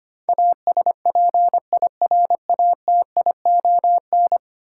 Morse code, heard as a keyed tone, about 25 wpm, 700 Hz. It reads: ASPIRATION
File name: 10 Letter Words - First 3 Letters Given - 25wpm